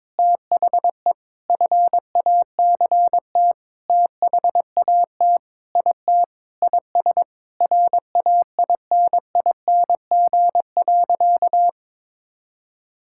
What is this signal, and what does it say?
Morse code, heard as a keyed tone, about 22 wpm, 700 Hz. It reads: THE FACT THAT IT IS RAINING.
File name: practice / 22wpm / Sentences with 8-Word Phrases - 22wpm